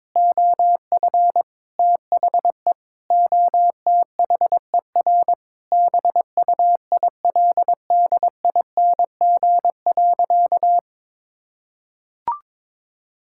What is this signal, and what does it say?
Morse code, heard as a keyed tone, about 22 wpm, 700 Hz. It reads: OF THE OTHER BUILDING.
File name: practice / 22wpm / Sentences with 3-Word Phrases - 22wpm